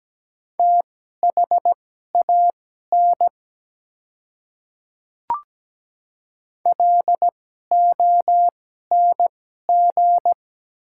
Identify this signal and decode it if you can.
Morse code, heard as a keyed tone, about 17 wpm, spaced as for 12 wpm, 700 Hz. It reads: THAN LONG
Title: 2x Character Spacing - Top 100 Words - 17wpm